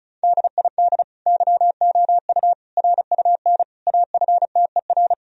Morse code, heard as a keyed tone, about 36 wpm, 700 Hz. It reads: DID YOU RUN AFTER
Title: Sentences from Top 1000 Words - Review Part 1 - 35wpm